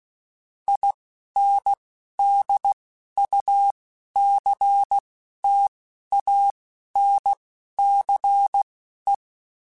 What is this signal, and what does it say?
Morse code, heard as a keyed tone, about 16 wpm, spaced as for 11 wpm, 800 Hz. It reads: INDUCTANCE